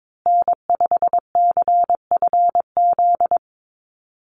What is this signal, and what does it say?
Morse code, heard as a keyed tone, about 22 wpm, 700 Hz. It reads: N5CFZ
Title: USA Call Signs Repeated 2x 22wpm